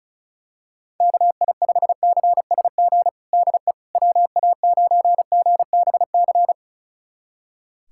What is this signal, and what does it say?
Morse code, heard as a keyed tone, about 35 wpm, 700 Hz. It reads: KI5CSG DE WA9GBC
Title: USA Callsign DE Callsign - 6 Characters - 35wpm